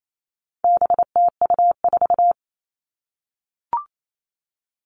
Morse code, heard as a keyed tone, about 28 wpm, 700 Hz. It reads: BTU4
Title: All Cut Numbers - 4-Digit 28wpm